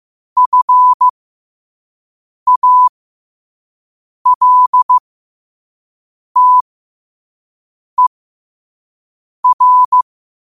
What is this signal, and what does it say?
Morse code, heard as a keyed tone, about 15 wpm, spaced as for 5 wpm, 1000 Hz. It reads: FALTER
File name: qotdfiles / 5wpm